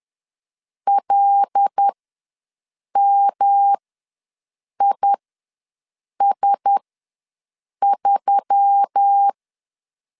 Morse code, heard as a keyed tone, about 10 wpm, spaced as for 6 wpm, 800 Hz. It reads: LMIS3
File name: LEZIONI / 6 PKX0 / 6 PK ws 10wpm 3745